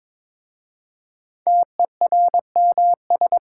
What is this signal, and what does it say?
Morse code, heard as a keyed tone, about 22 wpm, 700 Hz. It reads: TERMS
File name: 5 Letter Words - First 3 Letters Given - 22wpm